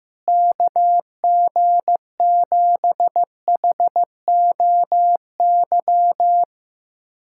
Morse code, heard as a keyed tone, about 16 wpm, 700 Hz. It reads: KG7HOY
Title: USA Callsign DE Callsign 15wpm